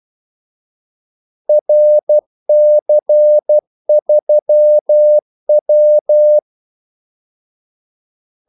Morse code, heard as a keyed tone, about 12 wpm, 600 Hz. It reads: RC3W
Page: 2 ekstra ordmellemrum mellem kaldesignalerne